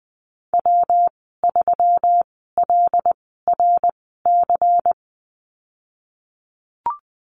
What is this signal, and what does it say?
Morse code, heard as a keyed tone, about 20 wpm, spaced as for 14 wpm, 700 Hz. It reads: W3LRC
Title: 147 Lesson 24g - Letter y - Call Signs 20wpm